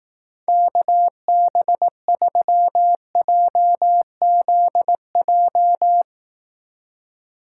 Morse code, heard as a keyed tone, about 18 wpm, 700 Hz. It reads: KB3JZJ
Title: Speed Racing - USA 2x3 Call Signs 15wpm